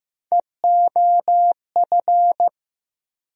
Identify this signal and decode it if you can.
Morse code, heard as a keyed tone, about 15 wpm, 700 Hz. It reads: EOF
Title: USA Call Sign Postfixes - 3 Letters 15wpm